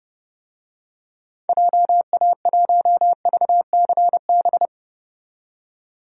Morse code, heard as a keyed tone, about 30 wpm, 700 Hz. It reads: JA1VCB
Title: DXCC-100 Call Signs - 6 Characters - 30wpm